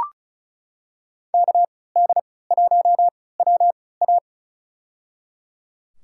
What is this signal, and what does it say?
Morse code, heard as a keyed tone, about 36 wpm, spaced as for 20 wpm, 700 Hz. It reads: KD1WA
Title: USA Call Signs - 5 Characters - 3x Character Spacing - 35wpm